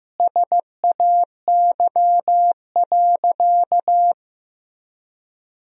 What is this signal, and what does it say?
Morse code, heard as a keyed tone, about 15 wpm, 700 Hz. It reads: SAY.
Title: Sentences from Top 1000 Words - Review Part 4 - 15wpm